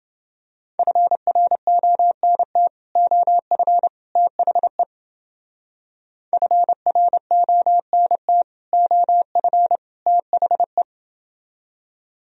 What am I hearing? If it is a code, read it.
Morse code, heard as a keyed tone, about 29 wpm, 700 Hz. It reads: FRONT OF THE FRONT OF THE